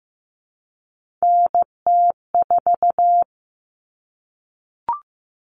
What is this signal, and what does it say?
Morse code, heard as a keyed tone, about 15 wpm, 700 Hz. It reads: NT4
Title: Common Cut Numbers - 3-Digit 15wpm